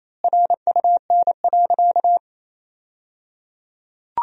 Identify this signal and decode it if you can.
Morse code, heard as a keyed tone, about 28 wpm, 700 Hz. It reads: RUN.
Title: 3x Word-Spacing - Sentences from Top 500 Words - 28wpm